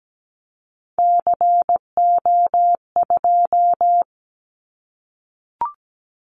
Morse code, heard as a keyed tone, about 17 wpm, 700 Hz. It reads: CO2
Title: Callsign Prefixes for Top 100 DXCC - 17wpm